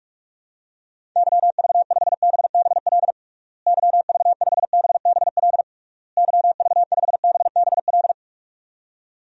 Morse code, heard as a keyed tone, about 46 wpm, 700 Hz. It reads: YV5BBL YV5BBL YV5BBL
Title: DXCC-100 Call Signs - 6 Characters - Repeated 3x - 45wpm